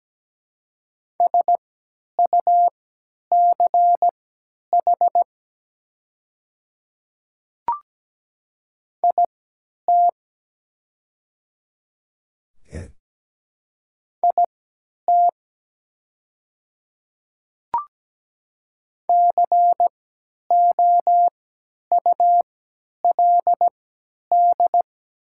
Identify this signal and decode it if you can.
Morse code, heard as a keyed tone, about 17 wpm, spaced as for 10 wpm, 700 Hz. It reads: SUCH IT IT COULD
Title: 3x Character Spacing - Top 200 Words - 17wpm